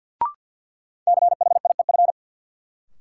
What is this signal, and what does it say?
Morse code, heard as a keyed tone, about 52 wpm, 700 Hz. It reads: CHIEF
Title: Sets of 5 Words Encore - New Word Drill - 50wpm